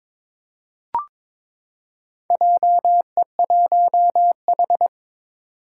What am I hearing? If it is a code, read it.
Morse code, heard as a keyed tone, about 22 wpm, 700 Hz. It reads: JE1H
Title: Four Challenging Letter-Number Combinations 22wpm